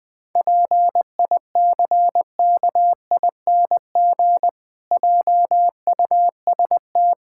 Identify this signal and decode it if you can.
Morse code, heard as a keyed tone, about 21 wpm, 700 Hz. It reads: PICKING JUST
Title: Pangram Challenge v2 - 20wpm